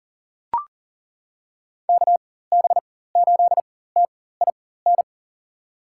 Morse code, heard as a keyed tone, about 38 wpm, spaced as for 18 wpm, 700 Hz. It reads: KB8TIN
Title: USA Call Signs - 6 Characters - 4x Character Spacing - 40wpm